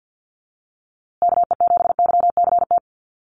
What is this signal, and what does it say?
Morse code, heard as a keyed tone, about 48 wpm, 700 Hz. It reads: KE7YCT
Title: USA Call Signs Repeated 2x 50wpm